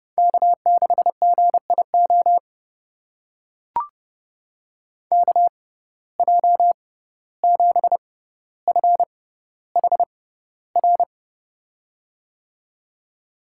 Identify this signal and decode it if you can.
Morse code, heard as a keyed tone, about 30 wpm, spaced as for 15 wpm, 700 Hz. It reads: K6GIO KJ7FHR
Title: USA Call Signs - 6x Character Spacing - 30wpm